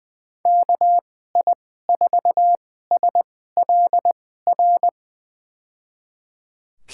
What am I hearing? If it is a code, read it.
Morse code, heard as a keyed tone, about 20 wpm, spaced as for 14 wpm, 700 Hz. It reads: KI4SLR
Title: USA Call Signs - 6 Characters - 2x Character Spacing - 20wpm